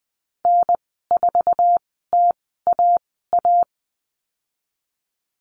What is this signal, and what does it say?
Morse code, heard as a keyed tone, about 20 wpm, spaced as for 15 wpm, 700 Hz. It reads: N4TAA
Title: 082 Lesson 14g - Number 2 - Call Signs 20wpm